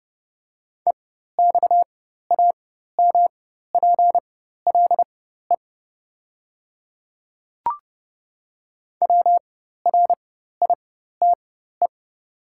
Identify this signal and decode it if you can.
Morse code, heard as a keyed tone, about 29 wpm, spaced as for 14 wpm, 690 Hz. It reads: EXAMPLE WRITE
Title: 4x Character Spacing - Top 300 Words - 30wpm